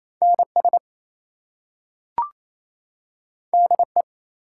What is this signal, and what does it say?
Morse code, heard as a keyed tone, about 28 wpm, 700 Hz. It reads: NS DE
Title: US States and Canadian Provinces 28wpm